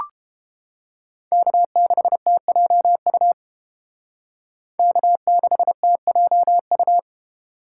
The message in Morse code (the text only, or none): K6TJU K6TJU